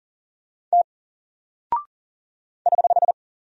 Morse code, heard as a keyed tone, about 40 wpm, 700 Hz. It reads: T <HH>
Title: Single Character (Letter:Number:Punctuation:Prosign) 40wpm